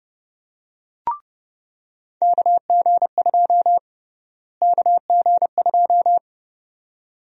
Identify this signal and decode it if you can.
Morse code, heard as a keyed tone, about 31 wpm, 700 Hz. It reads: KG2 KG2
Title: USA Call Sign Prefixes - Repeated 2x - 30wpm